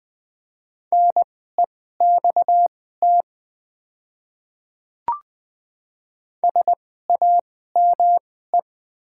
Morse code, heard as a keyed tone, about 20 wpm, spaced as for 14 wpm, 700 Hz. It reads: NEXT SAME
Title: 2x Character Spacing - Top 500 Words - 20wpm